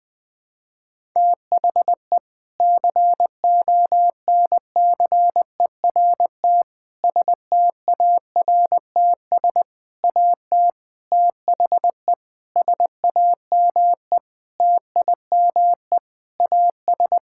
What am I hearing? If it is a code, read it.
Morse code, heard as a keyed tone, about 20 wpm, 700 Hz. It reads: THE CONCERT STARTS AT THE SAME TIME AS
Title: Sentences with 6-Word Phrases - 20wpm